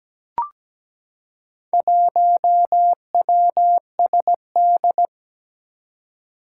Morse code, heard as a keyed tone, about 17 wpm, 700 Hz. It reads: 1WSD
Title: Four Challenging Letter-Number Combinations 17wpm